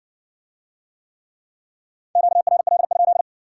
Morse code, heard as a keyed tone, about 60 wpm, 700 Hz. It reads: QRL?